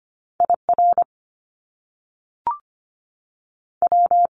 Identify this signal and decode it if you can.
Morse code, heard as a keyed tone, about 25 wpm, 700 Hz. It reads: I R W
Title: Top Two Letter Combinations in Words 25wpm